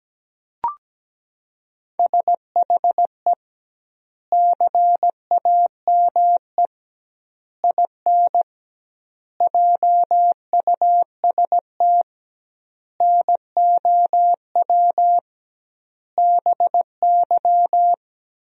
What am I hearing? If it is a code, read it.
Morse code, heard as a keyed tone, about 17 wpm, 700 Hz. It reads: SHE CAME IN JUST NOW BY
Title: 2x Word-Spacing - Sentences from Top 300 Words - 17wpm